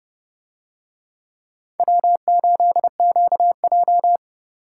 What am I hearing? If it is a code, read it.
Morse code, heard as a keyed tone, about 29 wpm, 700 Hz. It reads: W8QJ